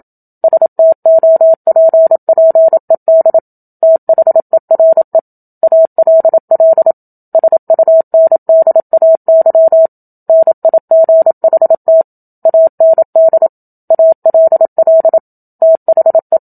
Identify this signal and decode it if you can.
Morse code, heard as a keyed tone, about 27 wpm, 650 Hz. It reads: STOPPED THERE ALL SUNDAY NIGHT AND ALL THE
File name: Book Two / 01-Chapter 1 - 27 WPM